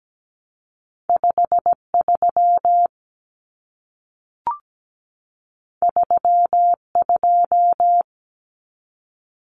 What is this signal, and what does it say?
Morse code, heard as a keyed tone, about 17 wpm, 700 Hz. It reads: 5 3 3 2
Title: Common Cut Numbers - 2-Digit 17wpm